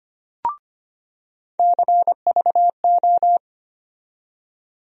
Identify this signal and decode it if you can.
Morse code, heard as a keyed tone, about 25 wpm, 700 Hz. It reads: CVO
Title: USA Call Sign Postfixes - 3 Letters 25wpm